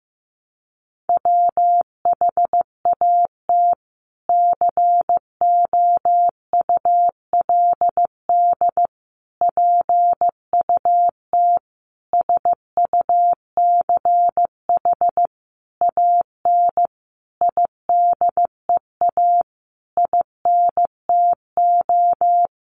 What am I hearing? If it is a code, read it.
Morse code, heard as a keyed tone, about 15 wpm, 700 Hz. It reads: WHAT COULD PUT SUCH AN IDEA INTO
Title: Sentences from Top 600 Words 15wpm